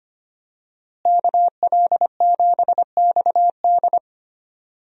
Morse code, heard as a keyed tone, about 25 wpm, 700 Hz. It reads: KL7XD
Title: DXCC-100 Call Signs - 5 Characters - Repeated 3x - 25wpm